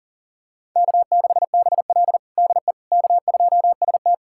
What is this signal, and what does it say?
Morse code, heard as a keyed tone, about 40 wpm, 700 Hz. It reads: K6BL DE K2ST
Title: USA Callsign DE Callsign - 4 Characters - 40wpm